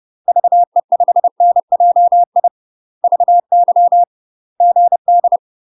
Morse code, heard as a keyed tone, about 30 wpm, 700 Hz. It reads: VE5NJI VY GD
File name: UK MASTER QSOs TRAINING / 30WPM / UK MASTER QSOs 0009